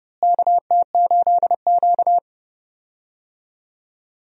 Morse code, heard as a keyed tone, about 30 wpm, 700 Hz. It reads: KT8Q
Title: USA Call Signs - 4 Characters - Repeated 2x - 30wpm